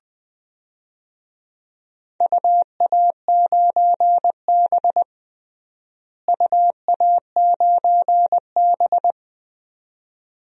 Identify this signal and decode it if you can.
Morse code, heard as a keyed tone, about 20 wpm, 700 Hz. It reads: UA9B UA9B
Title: DXCC-100 Call Signs - 4 Characters - Repeated 2x - 20wpm